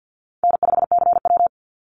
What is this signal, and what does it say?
Morse code, heard as a keyed tone, about 50 wpm, 700 Hz. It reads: N5CW
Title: Easy Call Signs - 50wpm